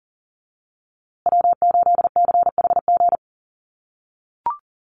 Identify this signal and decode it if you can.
Morse code, heard as a keyed tone, about 40 wpm, 700 Hz. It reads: W8CHG
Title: USA Call Signs Repeated 3x 40wpm